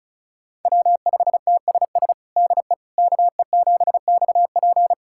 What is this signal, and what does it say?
Morse code, heard as a keyed tone, about 36 wpm, 700 Hz. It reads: W5TSS DE KE7XP
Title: USA Callsign DE Callsign 35wpm